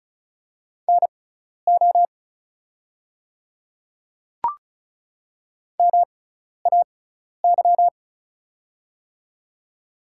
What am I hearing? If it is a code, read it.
Morse code, heard as a keyed tone, about 36 wpm, spaced as for 12 wpm, 690 Hz. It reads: NO MAY
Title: 6x Character Spacing - Top 100 Words - 35wpm